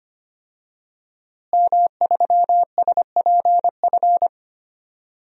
Morse code, heard as a keyed tone, about 25 wpm, 700 Hz. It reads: M3SPF
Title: DXCC-100 Call Signs - 5 Characters - Repeated 3x - 25wpm